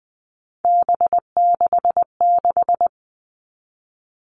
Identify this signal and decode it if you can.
Morse code, heard as a keyed tone, about 20 wpm, 700 Hz. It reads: B66
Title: All Cut Numbers - 3-Digit 20wpm